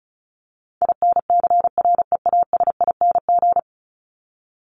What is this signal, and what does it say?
Morse code, heard as a keyed tone, about 34 wpm, 700 Hz. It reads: INCREASING
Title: ING Suffix - Top 100 Words with Base Words 35wpm